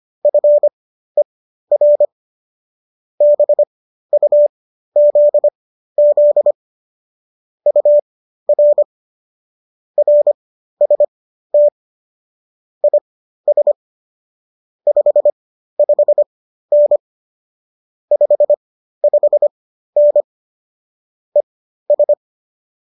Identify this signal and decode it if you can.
Morse code, heard as a keyed tone, about 25 wpm, spaced as for 13 wpm, 590 Hz. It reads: FER BUZZ UR RST IS 55N 55N ES
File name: qso207 13